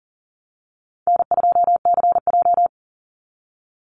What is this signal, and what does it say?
Morse code, heard as a keyed tone, about 40 wpm, 700 Hz. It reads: N2CJ
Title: Call Signs Repeated 3x - 40wpm